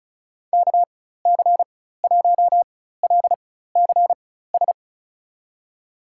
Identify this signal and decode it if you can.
Morse code, heard as a keyed tone, about 34 wpm, spaced as for 16 wpm, 700 Hz. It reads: KC1LCS
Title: USA Call Signs - 6 Characters - 4x Character Spacing - 35wpm